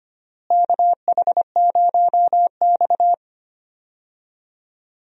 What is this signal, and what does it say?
Morse code, heard as a keyed tone, about 25 wpm, 700 Hz. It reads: KH0X
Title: USA Call Signs - 4 Characters - Repeated 2x - 25wpm